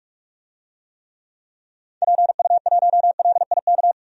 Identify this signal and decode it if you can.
Morse code, heard as a keyed tone, about 47 wpm, 700 Hz. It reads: PU1LIK
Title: DXCC-100 Call Signs - 6 Characters - 45wpm